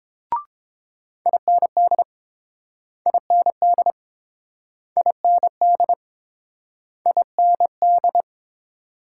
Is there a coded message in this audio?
Yes: Morse code from 1.3 to 8.2 s, heard as a keyed tone, about 30 wpm, 700 Hz. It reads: IND IND IND IND